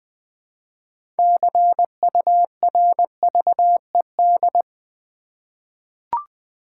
Morse code, heard as a keyed tone, about 20 wpm, 700 Hz. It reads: CURVED
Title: 6 Letter Words - First 3 Letters Given - 20wpm